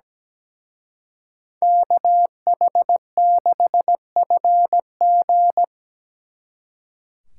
Morse code, heard as a keyed tone, about 20 wpm, 700 Hz. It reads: KH6FG
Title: Speed Racing - USA 2x2 Call Signs 17wpm